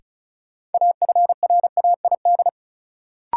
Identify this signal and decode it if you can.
Morse code, heard as a keyed tone, about 36 wpm, 700 Hz. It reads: AFRAID
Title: Sentences from Top 1500 Words - New Word Drill - 35wpm